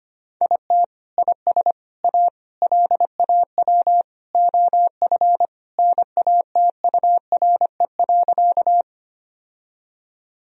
Morse code, heard as a keyed tone, about 25 wpm, 700 Hz. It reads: IT IS A LAW OF NATURE.
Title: Sentences from Top 1000 Words - Review Part 3 - 25wpm